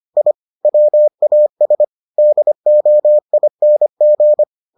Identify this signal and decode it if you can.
Morse code, heard as a keyed tone, about 25 wpm, 600 Hz. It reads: I WAS DOING